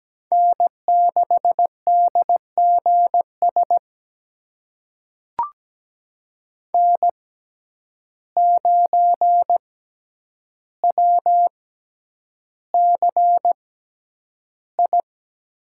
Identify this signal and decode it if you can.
Morse code, heard as a keyed tone, about 17 wpm, spaced as for 9 wpm, 700 Hz. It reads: N6DGS N9WCI
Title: USA Call Signs - 5 Characters - 6x Character Spacing - 17wpm